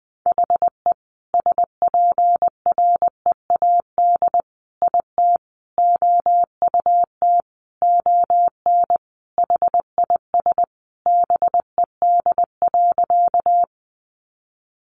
Morse code, heard as a keyed tone, about 19 wpm, 700 Hz. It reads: HE SPREAD IT OUT ON HIS BED.
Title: Sentences from Top 1200 Words 20wpm